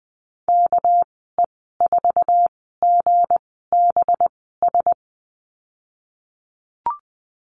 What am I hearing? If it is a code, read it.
Morse code, heard as a keyed tone, about 19 wpm, spaced as for 14 wpm, 700 Hz. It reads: KE4GBS
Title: 207 Lesson 33g - Letter j - Call Signs 20wpm